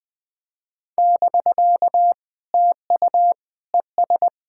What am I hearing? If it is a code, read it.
Morse code, heard as a keyed tone, about 20 wpm, 700 Hz. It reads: <BK> TU ES
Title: POTA QSO Practice - 20wpm